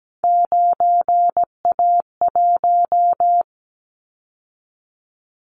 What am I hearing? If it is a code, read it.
Morse code, heard as a keyed tone, about 17 wpm, 700 Hz. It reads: 9A1
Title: Callsign Prefixes for Top 25 DXCC - 17wpm